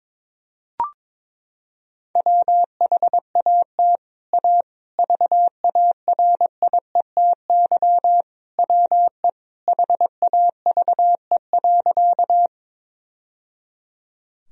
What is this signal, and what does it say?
Morse code, heard as a keyed tone, about 22 wpm, 700 Hz. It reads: WHAT A VARIETY WE HAVE.
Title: Sentences from Top 1000 Words - Review Part 4 - 22wpm